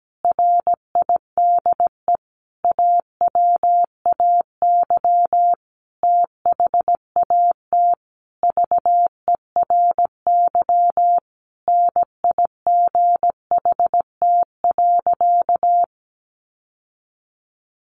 Morse code, heard as a keyed tone, about 17 wpm, 700 Hz. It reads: RIDE AWAY THAT VERY NIGHT.
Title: Sentences from Top 1200 Words 17wpm